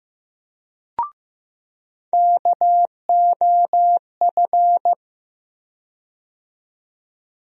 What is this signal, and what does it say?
Morse code, heard as a keyed tone, about 15 wpm, 700 Hz. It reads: KOF
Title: USA Call Sign Postfixes - 3 Letters 15wpm